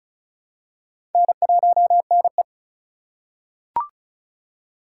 Morse code, heard as a keyed tone, about 34 wpm, 700 Hz. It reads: N1NE
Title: USA Call Signs - 4x Character Spacing - 35wpm